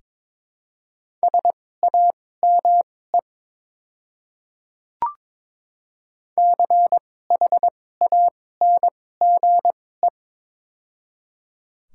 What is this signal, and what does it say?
Morse code, heard as a keyed tone, about 22 wpm, spaced as for 16 wpm, 700 Hz. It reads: SAME CHANGE
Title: 2x Character Spacing - Top 200 Words - 22wpm